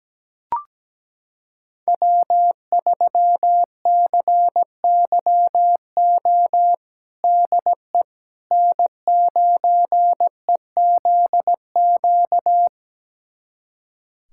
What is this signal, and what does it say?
Morse code, heard as a keyed tone, about 17 wpm, 700 Hz. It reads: W3CYO DE N9EZQ
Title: USA Callsign DE Callsign - 5 Characters - 17wpm